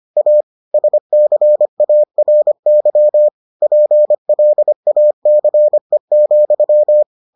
Morse code, heard as a keyed tone, about 25 wpm, 600 Hz. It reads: A SCARY PLACE,